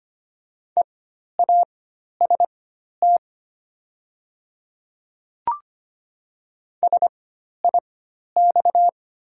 Morse code, heard as a keyed tone, about 25 wpm, spaced as for 12 wpm, 700 Hz. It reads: EAST SIX
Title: 4x Character Spacing - Top 500 Words - 25wpm